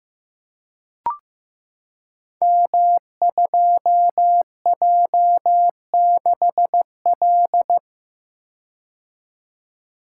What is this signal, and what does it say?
Morse code, heard as a keyed tone, about 15 wpm, 700 Hz. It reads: M2J6L
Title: Code Groups - 5 Letter:Numbers - 15wpm